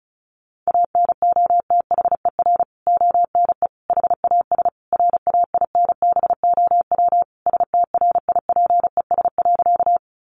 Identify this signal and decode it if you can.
Morse code, heard as a keyed tone, about 34 wpm, 700 Hz. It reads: ANOTHER ONE HAS RAINBOW STRIPES.
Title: Additional Easy Phrases 35wpm